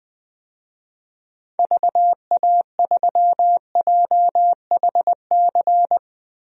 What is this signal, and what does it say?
Morse code, heard as a keyed tone, about 20 wpm, 700 Hz. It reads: VA3JHC
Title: DXCC-100 Call Signs - 6 Characters - 20wpm